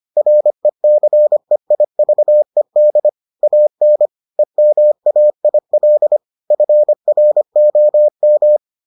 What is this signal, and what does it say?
Morse code, heard as a keyed tone, about 25 wpm, 600 Hz. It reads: RECEIVED AN EMAIL FROM